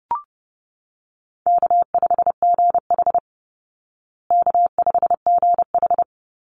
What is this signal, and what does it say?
Morse code, heard as a keyed tone, about 30 wpm, 700 Hz. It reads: K5GH K5GH
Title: Easy Call Signs Repeated 3x - 30wpm